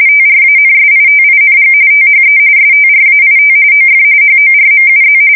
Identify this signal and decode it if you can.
RTTY, about 45.45 baud. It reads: TRANSITING THE STRAIT OF HORMUZ
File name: fsk170 / USS-Missouri